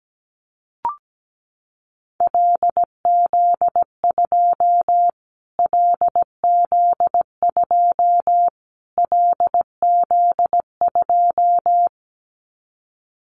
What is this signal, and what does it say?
Morse code, heard as a keyed tone, about 17 wpm, 700 Hz. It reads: LZ2 LZ2 LZ2
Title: Callsign Prefixes for Top 25 DXCC - Repeated 3x - 17wpm